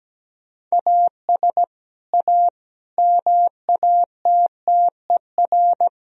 Morse code, heard as a keyed tone, about 17 wpm, 700 Hz. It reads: AS A MATTER
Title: Sentences from Top 1000 Words - Review Part 3 - 17wpm